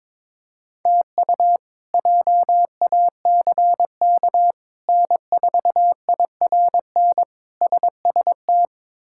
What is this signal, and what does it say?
Morse code, heard as a keyed tone, about 22 wpm, 700 Hz. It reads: TU JACK N4IRN SST
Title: K1USN SST QSO Practice - 22wpm